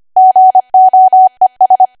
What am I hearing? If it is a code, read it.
Morse code, heard as a keyed tone, about 25 wpm, 750 Hz. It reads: GOES
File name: Archive / 25 WPM / 25WPM